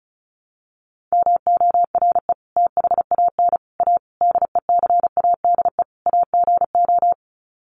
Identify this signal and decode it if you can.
Morse code, heard as a keyed tone, about 34 wpm, 700 Hz. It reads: MORE THAN A DECADE AGO
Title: Sets of 5 Words Encore2 35wpm